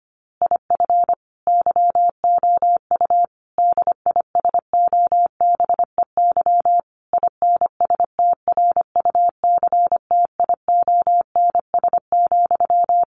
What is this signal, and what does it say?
Morse code, heard as a keyed tone, about 25 wpm, 700 Hz. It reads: IF YOU DISOBEY INSTRUCTIONS,
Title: A Sound Of Thunder 25wpm